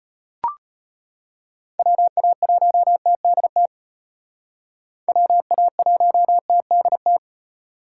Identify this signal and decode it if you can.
Morse code, heard as a keyed tone, about 32 wpm, 700 Hz. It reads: WA1TDT WA1TDT